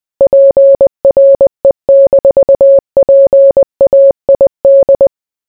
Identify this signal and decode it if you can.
Morse code, heard as a keyed tone, about 20 wpm, 550 Hz. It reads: PRE-PAID